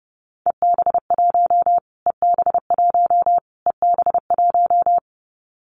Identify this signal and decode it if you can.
Morse code, heard as a keyed tone, about 30 wpm, 700 Hz. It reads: EB1 EB1 EB1
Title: Callsign Prefixes for Top 25 DXCC - Repeated 3x - 30wpm